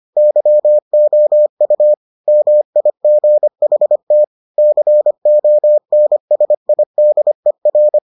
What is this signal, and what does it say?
Morse code, heard as a keyed tone, about 25 wpm, 600 Hz. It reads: YOU MIGHT CONSIDER